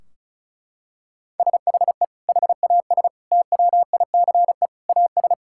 Morse code, heard as a keyed tone, about 36 wpm, 700 Hz. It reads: SHE HAS TWICE AS
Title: Sentences from Top 1600 Words 35wpm